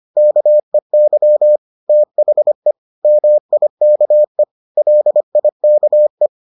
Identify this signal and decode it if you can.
Morse code, heard as a keyed tone, about 25 wpm, 600 Hz. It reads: KEY THE MIKE LIKE